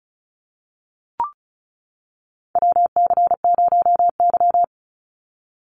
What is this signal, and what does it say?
Morse code, heard as a keyed tone, about 34 wpm, 700 Hz. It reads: WC0Y
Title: NA SOTA Chaser Callsigns 35wpm